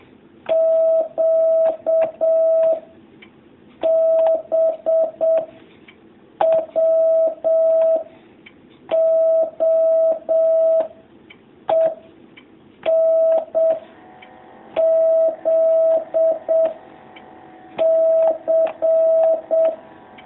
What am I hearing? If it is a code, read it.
Morse code, heard as a keyed tone, about 7 wpm, 650 Hz. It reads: Q B W O E N Z C
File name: jh8cbh / 2019cw